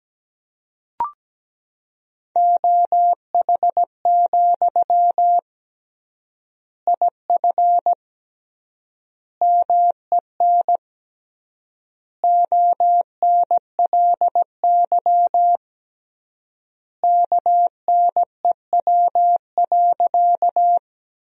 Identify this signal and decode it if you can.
Morse code, heard as a keyed tone, about 17 wpm, 700 Hz. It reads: OH, IF MEN ONLY KNEW.
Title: 3x Word-Spacing - Sentences from Top 500 Words - 17wpm